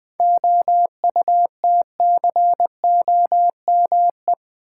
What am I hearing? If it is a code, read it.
Morse code, heard as a keyed tone, about 20 wpm, 700 Hz. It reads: OUTCOME
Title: Seven Letter Words - 20wpm